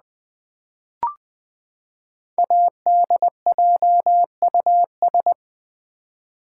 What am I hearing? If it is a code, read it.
Morse code, heard as a keyed tone, about 20 wpm, 700 Hz. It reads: ADJUS
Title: Build A Word - 5 Letters At A Time - 10-Letter Words - 20wpm